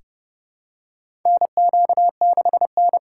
Morse code, heard as a keyed tone, about 30 wpm, 700 Hz. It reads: NQ6N
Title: USA Call Signs - 4 Characters - 2x Character Spacing - 30wpm